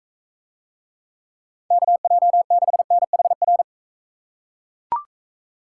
Morse code, heard as a keyed tone, about 42 wpm, 700 Hz. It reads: KJ6NHR